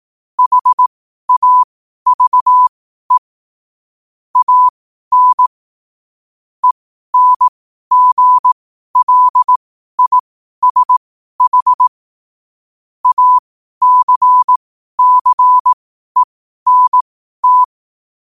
Morse code, heard as a keyed tone, about 18 wpm, spaced as for 12 wpm, 1000 Hz. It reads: HAVE AN ENGLISH ACCENT
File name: qotdfiles / 12wpm